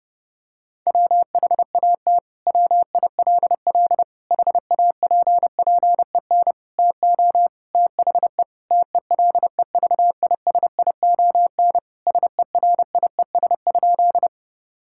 Morse code, heard as a keyed tone, about 30 wpm, 700 Hz. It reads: WHAT WILL HAPPEN TO THE TELEVISION SERIES?
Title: Sentences from Top 1000 Words - Review Part 5 - 30wpm